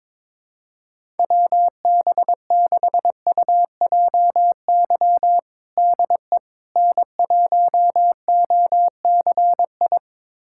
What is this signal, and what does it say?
Morse code, heard as a keyed tone, about 22 wpm, 700 Hz. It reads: WB6UJY DE N1OCI
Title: USA Callsign DE Callsign 22wpm